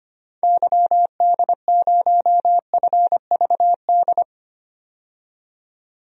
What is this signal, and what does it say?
Morse code, heard as a keyed tone, about 25 wpm, 700 Hz. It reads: YD0FVD
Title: DXCC-100 Call Signs - 6 Characters - 25wpm